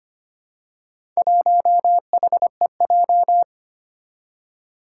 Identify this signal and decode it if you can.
Morse code, heard as a keyed tone, about 25 wpm, 700 Hz. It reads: 1HEJ
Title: Four Challenging Letter-Number Combinations 25wpm